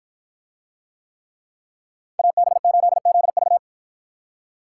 Morse code, heard as a keyed tone, about 53 wpm, 700 Hz. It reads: AB8ZV